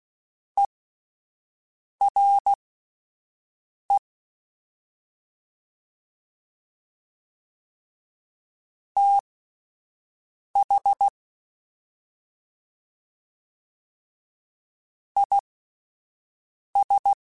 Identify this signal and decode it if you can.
Morse code, heard as a keyed tone, about 16 wpm, spaced as for 5 wpm, 800 Hz. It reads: ERE TH IS
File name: Misc / Misc-mp3s / Jumps @5